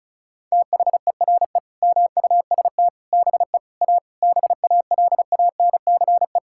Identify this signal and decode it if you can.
Morse code, heard as a keyed tone, about 36 wpm, 700 Hz. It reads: THERE MUST BE A BALANCE
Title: Sentences from Top 1000 Words - Review Part 6 - 35wpm